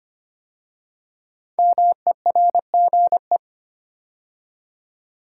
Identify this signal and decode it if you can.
Morse code, heard as a keyed tone, about 25 wpm, 700 Hz. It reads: MERGE
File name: 5 Letter Words - First 3 Letters Given - 25wpm